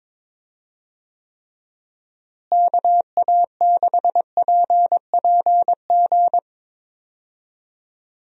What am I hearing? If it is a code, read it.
Morse code, heard as a keyed tone, about 22 wpm, 700 Hz. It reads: KA6PPG
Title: USA Call Signs - 6 Characters - Repeated 2x - 22wpm